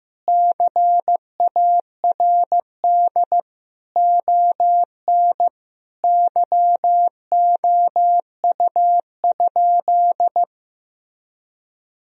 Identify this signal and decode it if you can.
Morse code, heard as a keyed tone, about 15 wpm, 700 Hz. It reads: CARD ON YOU?
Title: Sentences from Top 1000 Words - Review Part 4 - 15wpm